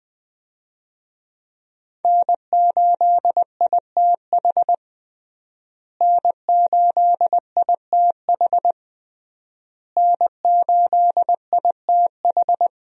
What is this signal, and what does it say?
Morse code, heard as a keyed tone, about 20 wpm, 700 Hz. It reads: N8ITH N8ITH N8ITH
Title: USA Call Signs - 5 Characters - Repeated 3x - 20wpm